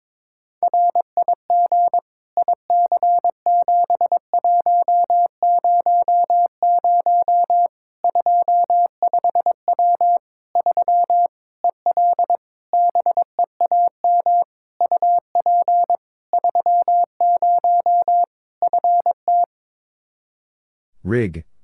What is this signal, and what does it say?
Morse code, heard as a keyed tone, about 22 wpm, 700 Hz. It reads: RIG IC7100 25W 3 EL BEAM UP 30 FT
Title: Practice QSO - Exchange 2 - Info - 22wpm